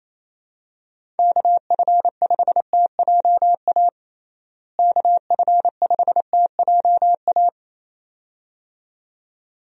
Morse code, heard as a keyed tone, about 27 wpm, 700 Hz. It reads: KF5TJA KF5TJA
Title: USA Call Signs - 6 Characters - Repeated 2x - 28wpm